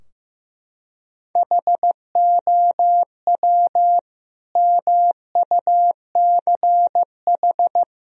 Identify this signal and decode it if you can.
Morse code, heard as a keyed tone, about 15 wpm, 700 Hz. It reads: HOW MUCH
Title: Sentences from Top 1000 Words - Review Part 4 - 15wpm